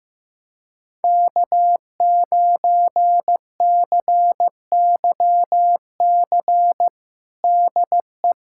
Morse code, heard as a keyed tone, about 15 wpm, 700 Hz. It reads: K9CYC DE
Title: USA Callsign DE Callsign - 5 Characters - 15wpm